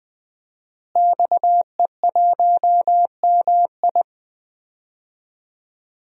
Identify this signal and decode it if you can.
Morse code, heard as a keyed tone, about 20 wpm, 700 Hz. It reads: XE1MI
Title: DXCC-100 Call Signs - 5 Characters - Repeated 2x - 20wpm